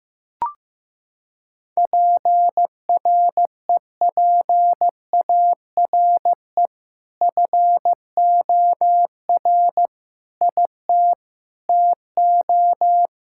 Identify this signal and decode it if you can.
Morse code, heard as a keyed tone, about 15 wpm, 700 Hz. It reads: PREPARE FOR IT TO
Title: Sentences from Top 1000 Words - Review Part 6 - 15wpm